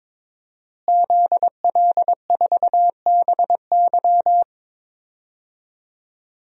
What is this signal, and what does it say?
Morse code, heard as a keyed tone, about 22 wpm, 700 Hz. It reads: ZL4BY
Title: DXCC-100 Call Signs - 5 Characters - Repeated 2x - 22wpm